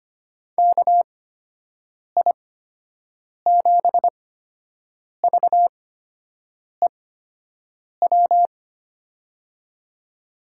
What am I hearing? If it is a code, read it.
Morse code, heard as a keyed tone, about 25 wpm, spaced as for 7 wpm, 700 Hz. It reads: KI7VEW
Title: USA Call Signs - 6 Characters - 8x Character Spacing - 25wpm